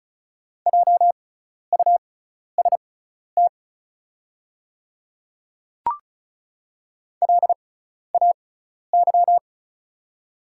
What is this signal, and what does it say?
Morse code, heard as a keyed tone, about 34 wpm, spaced as for 12 wpm, 700 Hz. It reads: JUST LAY
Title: 6x Character Spacing - Top 500 Words - 35wpm